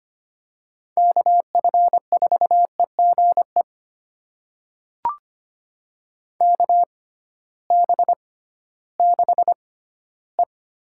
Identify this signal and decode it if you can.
Morse code, heard as a keyed tone, about 25 wpm, spaced as for 15 wpm, 700 Hz. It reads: KF4EGE KB6E
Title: USA Call Signs - 6 Characters - 6x Character Spacing - 25wpm